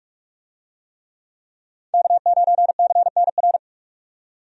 Morse code, heard as a keyed tone, about 46 wpm, 700 Hz. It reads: K9CNR
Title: USA Call Signs - 5 Characters - 45wpm